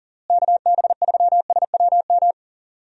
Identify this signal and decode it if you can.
Morse code, heard as a keyed tone, about 38 wpm, 700 Hz. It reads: KB3SWM
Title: USA Call Signs - 6 Characters - Repeated 3x - 40wpm